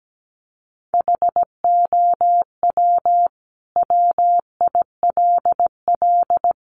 Morse code, heard as a keyed tone, about 17 wpm, 700 Hz. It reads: HOW WILL
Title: Sentences from Top 900 Words 17wpm